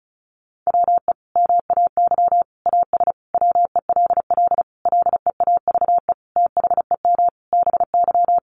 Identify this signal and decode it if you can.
Morse code, heard as a keyed tone, about 34 wpm, 700 Hz. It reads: WE MAY AS WELL LEAVE THEM BY
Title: Sentences from Top 800 Words 35wpm